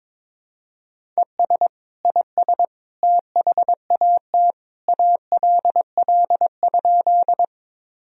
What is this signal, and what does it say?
Morse code, heard as a keyed tone, about 22 wpm, 700 Hz. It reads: ES IS THAT ALL?
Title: Sentences from Top 100 Words and Abbreviations 22wpm